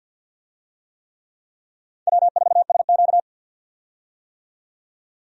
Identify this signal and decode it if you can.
Morse code, heard as a keyed tone, about 51 wpm, 700 Hz. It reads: W4SQ
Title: USA Call Signs - 4 Characters - 50wpm